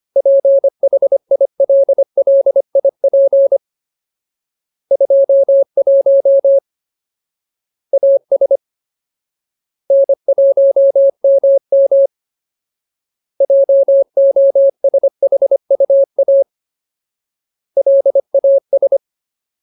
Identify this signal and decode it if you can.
Morse code, heard as a keyed tone, about 25 wpm, 550 Hz. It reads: PHILLIP 21 AS N1MM JOSHUA LAS
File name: Ultimate Word List / 25WPM Ultimate